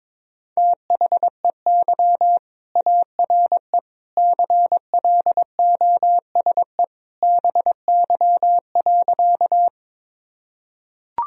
Morse code, heard as a keyed tone, about 22 wpm, 700 Hz. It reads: THEY ARE CLOSE BY.
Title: Sentences from Top 1000 Words - Review Part 1 - 22wpm